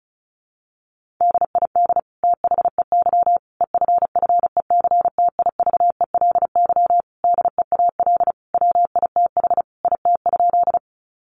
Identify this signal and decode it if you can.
Morse code, heard as a keyed tone, about 34 wpm, 700 Hz. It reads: DID THEY EFFECTIVELY DEAL WITH IT?
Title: Sentences from Top 1400 Words 35wpm